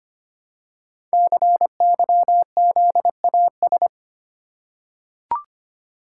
Morse code, heard as a keyed tone, about 25 wpm, 700 Hz. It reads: CYZAS
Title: Code Groups - 5 Letters - 25wpm